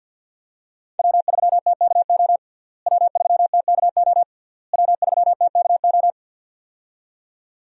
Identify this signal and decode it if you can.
Morse code, heard as a keyed tone, about 52 wpm, 700 Hz. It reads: W3TKO W3TKO W3TKO
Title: USA Call Signs - 5 Characters - Repeated 3x - 50wpm